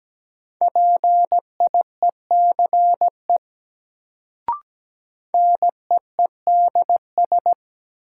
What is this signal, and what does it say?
Morse code, heard as a keyed tone, about 17 wpm, 700 Hz. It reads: PIECE NEEDS
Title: Sets of 3 Words - New Word Drill - 17wpm